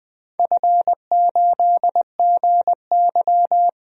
Morse code, heard as a keyed tone, about 21 wpm, 700 Hz. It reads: F8GY
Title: DXCC-100 Call Signs - 4 Characters - 20wpm